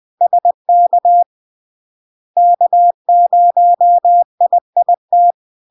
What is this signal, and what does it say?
Morse code, heard as a keyed tone, about 20 wpm, 700 Hz. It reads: SK K0IIT
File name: LARK-FD-2022-Log-in-CW-20wpm-700-Hz